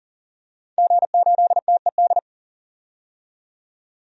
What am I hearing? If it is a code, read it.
Morse code, heard as a keyed tone, about 40 wpm, 700 Hz. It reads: G8TED
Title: DXCC-100 Call Signs - 5 Characters - 40wpm